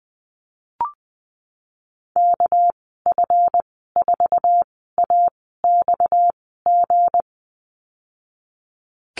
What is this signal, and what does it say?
Morse code, heard as a keyed tone, about 20 wpm, spaced as for 15 wpm, 700 Hz. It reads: KF4AXG
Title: 231 Lesson 37g - Letter x - Call Signs 20wpm